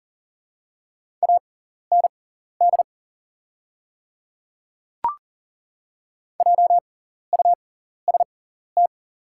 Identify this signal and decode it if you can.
Morse code, heard as a keyed tone, about 40 wpm, spaced as for 14 wpm, 700 Hz. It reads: AND JUST
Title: 6x Character Spacing - Top 200 Words - 40wpm